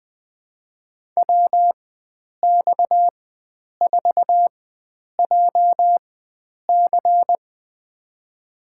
Morse code, heard as a keyed tone, about 21 wpm, spaced as for 9 wpm, 700 Hz. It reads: WX4JC
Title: USA Call Signs - 5 Characters - 4x Character Spacing - 20wpm